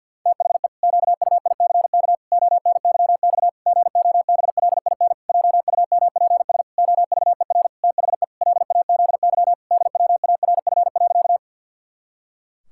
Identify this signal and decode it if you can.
Morse code, heard as a keyed tone, about 50 wpm, 700 Hz. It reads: THE QUICK ONYX GOBLIN JUMPS OVER THE LAZY DWARF.
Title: Pangram Challenge v2 - 50wpm